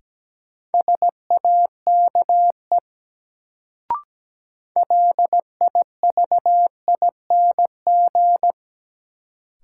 Morse code, heard as a keyed tone, about 17 wpm, 700 Hz. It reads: SAKE LIVING
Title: Sets of 4 Words - New Word Drill - 17wpm